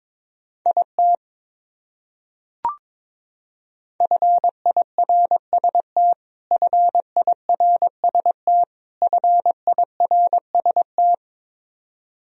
Morse code, heard as a keyed tone, about 22 wpm, 700 Hz. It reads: IT FIRST FIRST FIRST
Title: Top 100 Words Repeated 22wpm